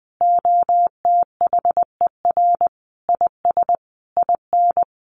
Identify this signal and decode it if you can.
Morse code, heard as a keyed tone, about 19 wpm, 700 Hz. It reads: OTHER IS IN
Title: Sentences from Top 1500 Words 20wpm